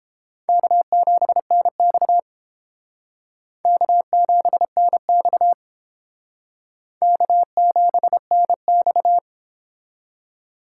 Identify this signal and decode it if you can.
Morse code, heard as a keyed tone, about 31 wpm, 700 Hz. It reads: K7NX K7NX K7NX